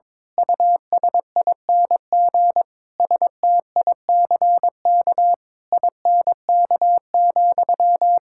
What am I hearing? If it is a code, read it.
Morse code, heard as a keyed tone, about 22 wpm, 700 Hz. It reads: USING STICK INK,
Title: Pangram Challenge v2 - 22wpm